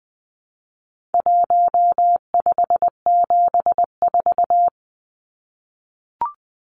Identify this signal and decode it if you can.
Morse code, heard as a keyed tone, about 20 wpm, 700 Hz. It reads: 1574
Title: Numbers - 4-Digit 20wpm